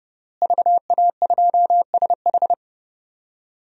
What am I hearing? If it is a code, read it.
Morse code, heard as a keyed tone, about 30 wpm, 700 Hz. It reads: VA2SH
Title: DXCC-100 Call Signs - 5 Characters - Repeated 3x - 30wpm